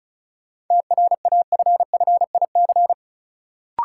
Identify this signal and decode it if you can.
Morse code, heard as a keyed tone, about 34 wpm, 700 Hz. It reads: TRAFFIC
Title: Sentences from Top 1300 Words - New Word Drill Alt - 35wpm